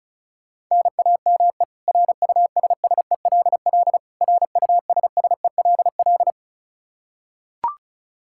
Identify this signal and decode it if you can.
Morse code, heard as a keyed tone, about 36 wpm, 700 Hz. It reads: NAME RUSSELL RUSSELL
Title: K1USN SST QSO Practice 35wpm